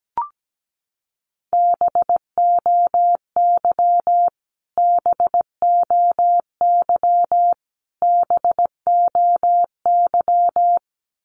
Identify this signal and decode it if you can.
Morse code, heard as a keyed tone, about 17 wpm, 700 Hz. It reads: BOY BOY BOY
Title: Top 200 Words Repeated 17wpm